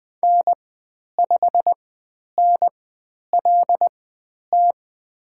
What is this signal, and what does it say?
Morse code, heard as a keyed tone, about 20 wpm, spaced as for 10 wpm, 700 Hz. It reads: N5NLT
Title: 109 Lesson 18g - Letter m - Call Signs 20wpm